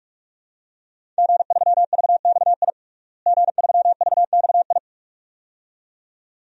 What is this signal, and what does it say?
Morse code, heard as a keyed tone, about 42 wpm, 700 Hz. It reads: G3VXI G3VXI
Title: DXCC-100 Call Signs - 5 Characters - Repeated 2x - 45wpm